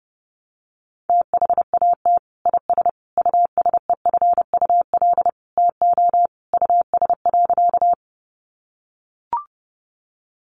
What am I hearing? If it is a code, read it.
Morse code, heard as a keyed tone, about 30 wpm, 700 Hz. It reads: THAT IS USEFUL TO US.
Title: Sentences from Top 700 Words 30wpm